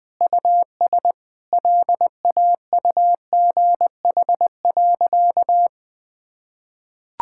Morse code, heard as a keyed tone, about 20 wpm, 700 Hz. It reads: US LAUGH.
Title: Sentences from Top 1000 Words - Review Part 5 - 20wpm